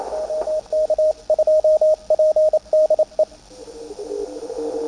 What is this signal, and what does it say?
Morse code, heard as a keyed tone, about 27 wpm, 620 Hz, with radio noise. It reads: OK2PDE
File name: attachments / CW 20m